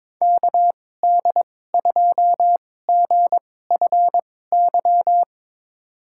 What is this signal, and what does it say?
Morse code, heard as a keyed tone, about 22 wpm, spaced as for 16 wpm, 700 Hz. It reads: KD2GFY
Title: USA Call Signs - 2x Character Spacing - 22wpm